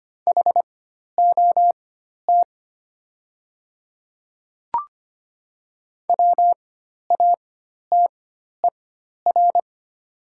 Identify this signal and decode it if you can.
Morse code, heard as a keyed tone, about 25 wpm, spaced as for 12 wpm, 700 Hz. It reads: HOT WATER
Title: 4x Character Spacing - Top 300 Words - 25wpm